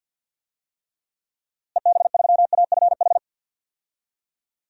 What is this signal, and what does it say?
Morse code, heard as a keyed tone, about 50 wpm, 700 Hz. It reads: EB3AFH
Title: DXCC-100 Call Signs - 6 Characters - Repeated 2x - 50wpm